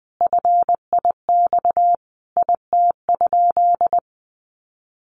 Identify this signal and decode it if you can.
Morse code, heard as a keyed tone, about 20 wpm, 700 Hz. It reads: FIX IT?
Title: Sentences from Top 1400 Words 20wpm